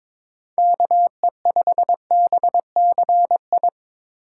Morse code, heard as a keyed tone, about 22 wpm, 700 Hz. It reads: KE5BCI
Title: USA Call Signs - 4x Character Spacing - 22wpm